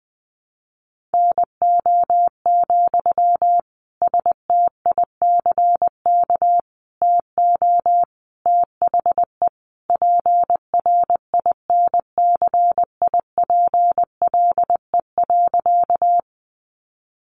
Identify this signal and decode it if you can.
Morse code, heard as a keyed tone, about 19 wpm, 700 Hz. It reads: NO, STICK TO THE PRINCIPLE.
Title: Sentences from Top 1400 Words 20wpm